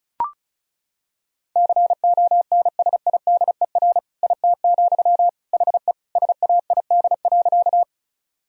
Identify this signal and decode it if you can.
Morse code, heard as a keyed tone, about 36 wpm, 700 Hz. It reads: CONSIDER IT, HE SAID.
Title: Sentences from Top 1000 Words - Review Part 4 - 35wpm